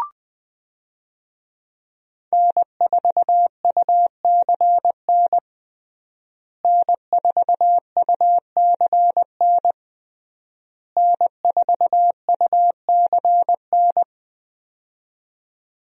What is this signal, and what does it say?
Morse code, heard as a keyed tone, about 20 wpm, 700 Hz. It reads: N4UCN N4UCN N4UCN
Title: USA Call Signs - 5 Characters - Repeated 3x - 20wpm